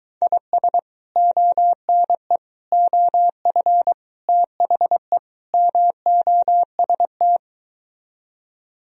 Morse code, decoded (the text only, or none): IS ONE OF THE MOST